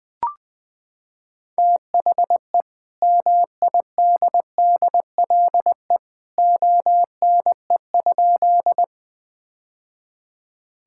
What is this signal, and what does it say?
Morse code, heard as a keyed tone, about 20 wpm, 700 Hz. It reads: THE MIDDLE ONE?
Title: Sentences from Top 1000 Words - Review Part 6 - 20wpm